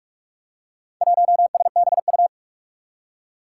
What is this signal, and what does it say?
Morse code, heard as a keyed tone, about 44 wpm, 700 Hz. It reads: 1SBU
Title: Four Challenging Letter-Number Combinations 45wpm